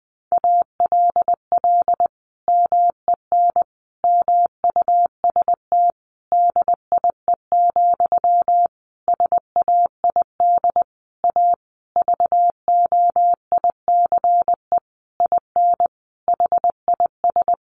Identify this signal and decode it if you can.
Morse code, heard as a keyed tone, about 20 wpm, 700 Hz. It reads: ALL MEN MUST DIE, SAID A VOICE IN HIS
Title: Sentences from Top 1300 Words 20wpm